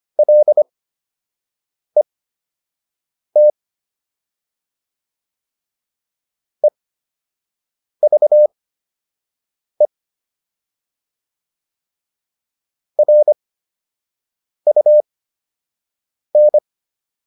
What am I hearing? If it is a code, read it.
Morse code, heard as a keyed tone, about 25 wpm, spaced as for 6 wpm, 600 Hz. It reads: LET EVE RUN